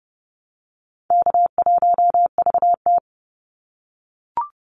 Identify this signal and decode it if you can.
Morse code, heard as a keyed tone, about 30 wpm, 700 Hz. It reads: K1VT
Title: Call Signs - 30wpm